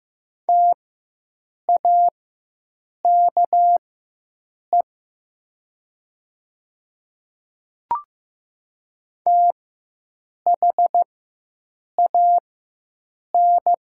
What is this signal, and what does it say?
Morse code, heard as a keyed tone, about 15 wpm, spaced as for 7 wpm, 700 Hz. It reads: TAKE THAN
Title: 4x Character Spacing - Top 500 Words - 15wpm